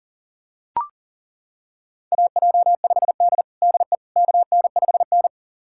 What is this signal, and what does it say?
Morse code, heard as a keyed tone, about 40 wpm, 700 Hz. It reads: AJ5D DE KN5N
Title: USA Callsign DE Callsign - 4 Characters - 40wpm